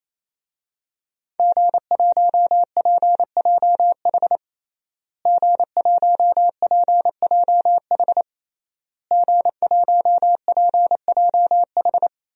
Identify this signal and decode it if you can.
Morse code, heard as a keyed tone, about 28 wpm, 700 Hz. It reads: G1PJH G1PJH G1PJH
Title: DXCC-100 Call Signs - 5 Characters - Repeated 3x - 28wpm